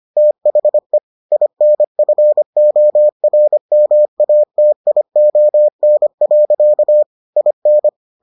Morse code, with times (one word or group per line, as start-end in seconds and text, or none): 0.2-1.0 THE
1.3-7.0 INFORMATION.
7.4-7.9 IN